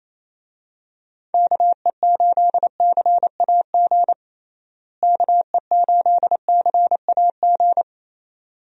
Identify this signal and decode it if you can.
Morse code, heard as a keyed tone, about 28 wpm, 700 Hz. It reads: KE8CAG KE8CAG
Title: USA Call Signs - 6 Characters - Repeated 2x - 28wpm